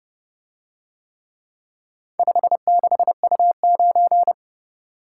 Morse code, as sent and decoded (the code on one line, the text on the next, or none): ..... -.... ..- ----.
56U9